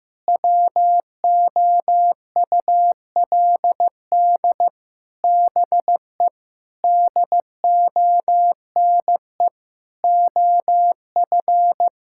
Morse code, heard as a keyed tone, about 15 wpm, 700 Hz. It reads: WOULD BE DONE OF
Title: Sentences from Top 1000 Words - Review Part 5 - 15wpm